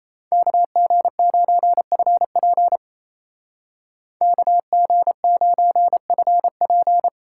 Morse code, heard as a keyed tone, about 33 wpm, 700 Hz. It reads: KG9FP KG9FP
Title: Speed Racing - USA Call Signs 28wpm